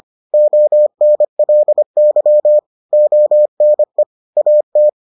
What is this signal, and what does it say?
Morse code, heard as a keyed tone, about 25 wpm, 600 Hz. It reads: ONLY ONE AT